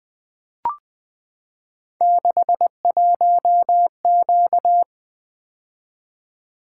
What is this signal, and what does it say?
Morse code, heard as a keyed tone, about 20 wpm, 700 Hz. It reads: 61Q
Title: Three Letter-Number Combinations - 20wpm